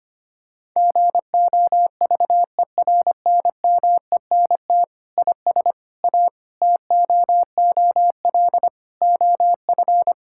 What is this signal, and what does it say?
Morse code, heard as a keyed tone, about 25 wpm, 700 Hz. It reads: GOVERNMENT IS A TOOL OF
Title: Sentences from Top 1000 Words - Review Part 6 - 25wpm